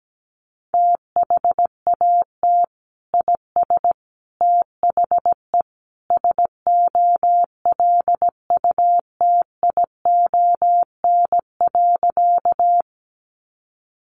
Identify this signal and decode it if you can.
Morse code, heard as a keyed tone, about 17 wpm, 700 Hz. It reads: THAT IS THE SOLUTION.
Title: Sentences from Top 1300 Words 17wpm